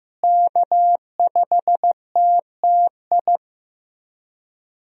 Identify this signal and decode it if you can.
Morse code, heard as a keyed tone, about 16 wpm, 700 Hz. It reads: K5TTI
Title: USA Call Signs - 5 Characters - Repeated 3x - 15wpm